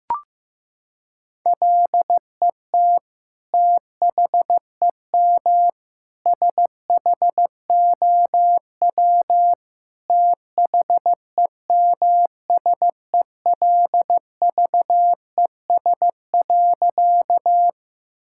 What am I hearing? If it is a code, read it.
Morse code, heard as a keyed tone, about 14 wpm, 700 Hz. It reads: LET THEM SHOW THEMSELVES.
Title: Sentences from Top 1000 Words - Review Part 5 - 15wpm